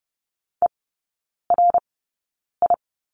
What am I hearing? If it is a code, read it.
Morse code, heard as a keyed tone, about 30 wpm, spaced as for 9 wpm, 700 Hz. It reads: ERI
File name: 081 Lesson 14e - Number 2 - Three Characters 30wpm